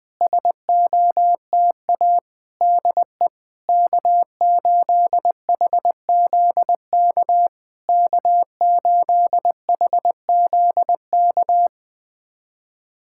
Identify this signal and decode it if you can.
Morse code, heard as a keyed tone, about 20 wpm, 690 Hz. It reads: SOTA DE K8HZK K8HZK
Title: SOTA QSO Practice 20wpm